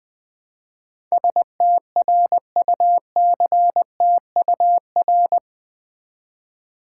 Morse code, heard as a keyed tone, about 20 wpm, 700 Hz. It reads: STRUCTUR
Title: Build A Word - 4 Letters At A Time - 10-Letter Words - 20wpm